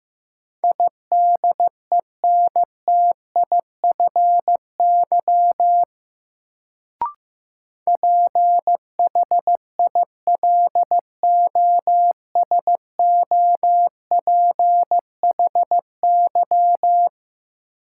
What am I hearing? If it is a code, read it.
Morse code, heard as a keyed tone, about 15 wpm, 700 Hz. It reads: IDENTIFY PHILOSOPHY
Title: Sentences from Top 900 Words - New Word Drill - 15wpm